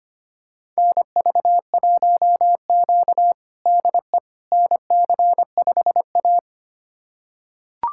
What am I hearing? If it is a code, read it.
Morse code, heard as a keyed tone, about 25 wpm, 700 Hz. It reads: NV1Q DE NC5A
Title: USA Callsign DE Callsign - 4 Characters - 25wpm